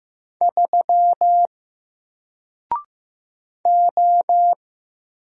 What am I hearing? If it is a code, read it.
Morse code, heard as a keyed tone, about 15 wpm, 700 Hz. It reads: 3 O
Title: Single Letter-Number 15wpm